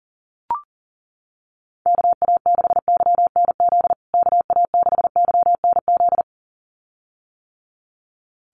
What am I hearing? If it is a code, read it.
Morse code, heard as a keyed tone, about 40 wpm, 700 Hz. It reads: KA6YNZ KA6YNZ
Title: USA Call Signs Repeated 2x 40wpm